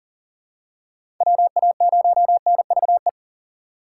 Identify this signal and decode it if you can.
Morse code, heard as a keyed tone, about 39 wpm, 700 Hz. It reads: WA0NVE
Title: USA Call Signs - 6 Characters - 2x Character Spacing - 40wpm